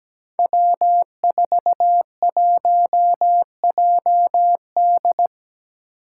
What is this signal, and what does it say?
Morse code, heard as a keyed tone, about 17 wpm, 700 Hz. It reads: W41JD
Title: Five Challenging Letter-Number Combinations 17wpm